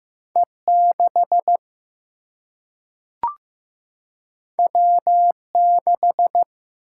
Morse code, heard as a keyed tone, about 15 wpm, 700 Hz. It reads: E 6 W 6
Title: Single-Letter Single-Number 15wpm